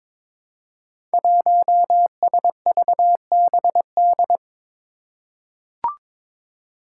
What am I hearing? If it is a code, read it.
Morse code, heard as a keyed tone, about 22 wpm, 700 Hz. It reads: 1SVBD
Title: Five Challenging Letter-Number Combinations 22wpm